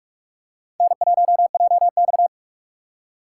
Morse code, heard as a keyed tone, about 46 wpm, 700 Hz. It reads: N1JX
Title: USA Call Signs - 4 Characters - 4x Character Spacing - 45wpm